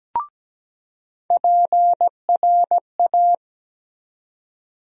Morse code, heard as a keyed tone, about 17 wpm, 700 Hz. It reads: PRA
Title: USA Call Sign Postfixes - 3 Letters 17wpm